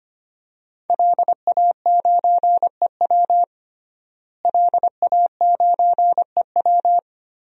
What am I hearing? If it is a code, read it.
Morse code, heard as a keyed tone, about 25 wpm, 700 Hz. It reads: LA9EW LA9EW
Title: DXCC-100 Call Signs - 5 Characters - Repeated 2x - 25wpm